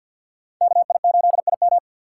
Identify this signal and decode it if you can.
Morse code, heard as a keyed tone, about 47 wpm, 690 Hz. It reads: KI8IM
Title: USA Call Signs - 5 Characters - 6x Character Spacing - 50wpm